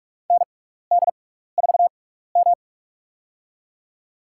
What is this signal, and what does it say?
Morse code, heard as a keyed tone, about 46 wpm, spaced as for 16 wpm, 700 Hz. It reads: ND4M
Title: USA Call Signs - 6x Character Spacing - 45wpm